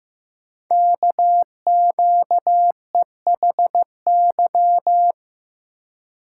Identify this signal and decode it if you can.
Morse code, heard as a keyed tone, about 15 wpm, 700 Hz. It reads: KQEHY
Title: Code Groups - 5 Letters - 15wpm